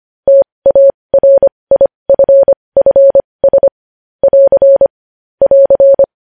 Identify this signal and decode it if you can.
Morse code, heard as a keyed tone, about 25 wpm, spaced as for 20 wpm, 550 Hz. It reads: TARIFFS <AR> <AR>
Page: news-headlines-in-morse-code-at-20-wpm